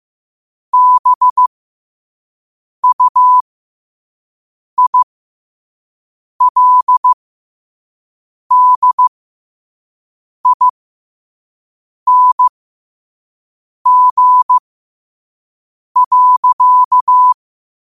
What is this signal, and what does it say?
Morse code, heard as a keyed tone, about 15 wpm, spaced as for 5 wpm, 1000 Hz. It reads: BUILDING.
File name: qotdfiles / 5wpm